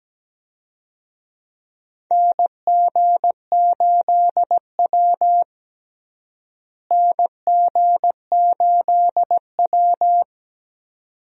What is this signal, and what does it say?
Morse code, heard as a keyed tone, about 17 wpm, 700 Hz. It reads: NG8W NG8W
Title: USA Call Signs - 4 Characters - Repeated 2x - 17wpm